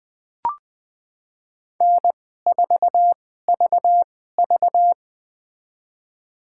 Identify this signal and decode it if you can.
Morse code, heard as a keyed tone, about 20 wpm, spaced as for 15 wpm, 700 Hz. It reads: N4VV
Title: USA Call Signs - 4 Characters - 2x Character Spacing - 20wpm